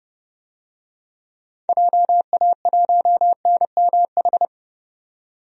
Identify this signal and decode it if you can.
Morse code, heard as a keyed tone, about 30 wpm, 700 Hz. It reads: JA1NMH
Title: DXCC-100 Call Signs - 6 Characters - 30wpm